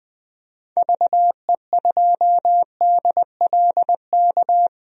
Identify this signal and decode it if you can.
Morse code, heard as a keyed tone, about 20 wpm, 700 Hz. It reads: VE2DLK
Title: DXCC-100 Call Signs - 6 Characters - 20wpm